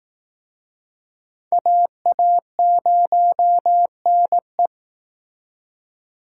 Morse code, heard as a keyed tone, about 18 wpm, 700 Hz. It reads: AA0NE
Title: Speed Racing - USA 2x2 Call Signs 15wpm